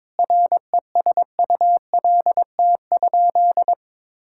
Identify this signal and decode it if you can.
Morse code, heard as a keyed tone, about 22 wpm, 700 Hz. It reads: RESULT?
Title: Sentences from Top 1000 Words - Review Part 5 - 22wpm